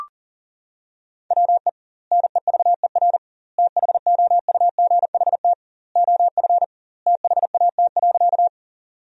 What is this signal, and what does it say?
Morse code, heard as a keyed tone, about 40 wpm, 700 Hz. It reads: WE NEVER THOUGHT OF THAT.
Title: 2x Word-Spacing - Sentences from Top 300 Words - 40wpm